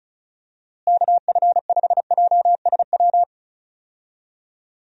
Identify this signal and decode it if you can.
Morse code, heard as a keyed tone, about 34 wpm, 700 Hz. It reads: KF5JSW
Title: USA Call Signs - 6 Characters - Repeated 3x - 35wpm